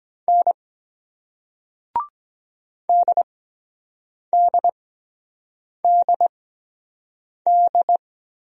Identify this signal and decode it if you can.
Morse code, heard as a keyed tone, about 23 wpm, spaced as for 7 wpm, 700 Hz. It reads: N DDDD